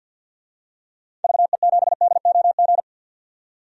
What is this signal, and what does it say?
Morse code, heard as a keyed tone, about 48 wpm, 700 Hz. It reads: VE7DOG
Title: DXCC-100 Call Signs - 6 Characters - 50wpm